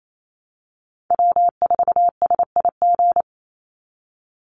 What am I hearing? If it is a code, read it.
Morse code, heard as a keyed tone, about 28 wpm, 700 Hz. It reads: W4SIG
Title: Call Signs Repeated 3x - 28wpm